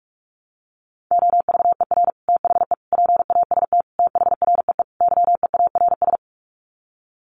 Morse code, heard as a keyed tone, about 44 wpm, 700 Hz. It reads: OVER THE PAST THREE YEARS
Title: Sets of 5 Words Encore2 45wpm